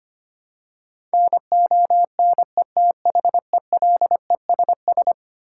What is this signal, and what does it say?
Morse code, heard as a keyed tone, about 25 wpm, 700 Hz. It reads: NONETHELESS
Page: morse-code-ninja-practice-25wpm